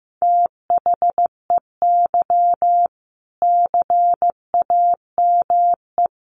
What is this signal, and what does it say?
Morse code, heard as a keyed tone, about 15 wpm, 700 Hz. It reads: THEY CAME
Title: Sentences from Top 300 Words 15wpm